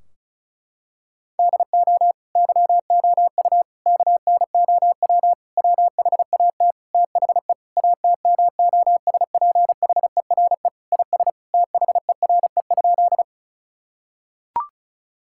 Morse code, heard as a keyed tone, about 36 wpm, 700 Hz. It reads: DO YOU KNOW WHAT THE ATMOSPHERE IS THERE?
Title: Sentences from Top 1600 Words 35wpm